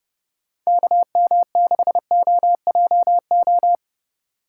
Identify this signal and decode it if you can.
Morse code, heard as a keyed tone, about 32 wpm, 700 Hz. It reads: KM6OJO